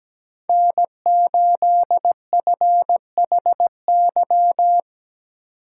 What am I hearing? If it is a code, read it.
Morse code, heard as a keyed tone, about 17 wpm, 700 Hz. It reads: N8FHY
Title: USA Call Signs - 5 Characters - 2x Character Spacing - 17wpm